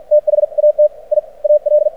Morse code, heard as a keyed tone, about 53 wpm, 600 Hz. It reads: TH AT I AL
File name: archived / CRSnet